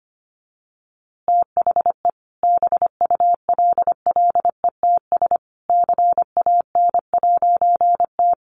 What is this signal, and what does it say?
Morse code, heard as a keyed tone, about 25 wpm, 700 Hz. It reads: THE BULLETS CAN'T
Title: A Sound Of Thunder 25wpm